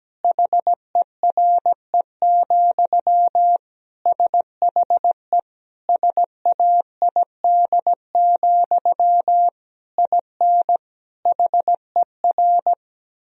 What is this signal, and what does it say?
Morse code, heard as a keyed tone, about 17 wpm, 700 Hz. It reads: HERE, SHE SAID, IN HER
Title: Sentences from Top 1000 Words - Review Part 1 - 17wpm